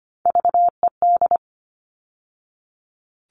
Morse code, heard as a keyed tone, about 25 wpm, 700 Hz. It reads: VED
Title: All Cut Numbers - 3-Digit 25wpm